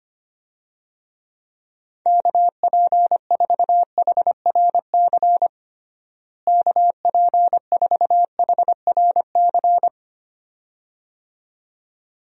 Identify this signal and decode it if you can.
Morse code, heard as a keyed tone, about 25 wpm, 700 Hz. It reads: KP4HRC KP4HRC
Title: USA Call Signs - 6 Characters - Repeated 2x - 25wpm